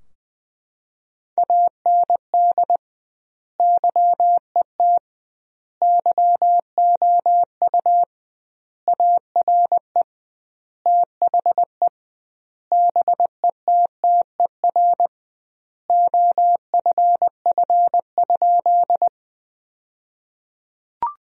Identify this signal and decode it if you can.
Morse code, heard as a keyed tone, about 21 wpm, 700 Hz. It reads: AND YET YOU ARE THE BETTER OFF?
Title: 2x Word-Spacing - Sentences from Top 500 Words - 20wpm